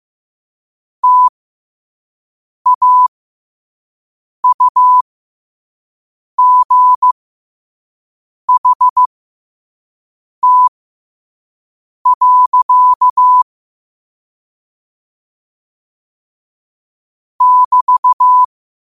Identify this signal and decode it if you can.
Morse code, heard as a keyed tone, about 15 wpm, spaced as for 5 wpm, 1000 Hz. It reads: TAUGHT. <BT>
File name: qotdfiles / 5wpm